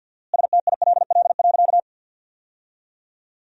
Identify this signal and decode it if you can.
Morse code, heard as a keyed tone, about 50 wpm, 700 Hz. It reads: STILL.
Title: 4x Word-Spacing - Sentences from Top 500 Words - 50wpm